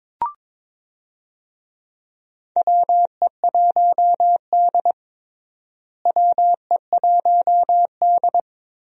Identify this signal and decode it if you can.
Morse code, heard as a keyed tone, about 22 wpm, 700 Hz. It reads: WE1D WE1D
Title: USA Call Signs - 4 Characters - Repeated 3x - 22wpm